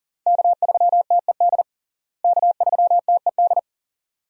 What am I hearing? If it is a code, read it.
Morse code, heard as a keyed tone, about 37 wpm, 700 Hz. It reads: K3TED K3TED
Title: USA Call Signs - 5 Characters - Repeated 2x - 40wpm